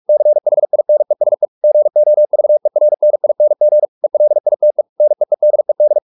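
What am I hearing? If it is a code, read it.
Morse code, heard as a keyed tone, about 44 wpm, 600 Hz. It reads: CHINESE GOVERNING ELITE NEEDED
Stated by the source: Morse practice at 45 wpm, the novel Never by Ken Follett in English, from CW med Gnister.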